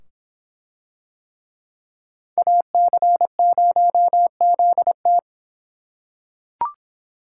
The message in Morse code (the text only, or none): AC0ZT